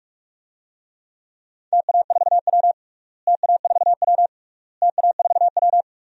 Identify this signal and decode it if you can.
Morse code, heard as a keyed tone, about 46 wpm, 700 Hz. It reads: TA4W TA4W TA4W
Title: DXCC-100 Call Signs - 4 Characters - Repeated 3x - 45wpm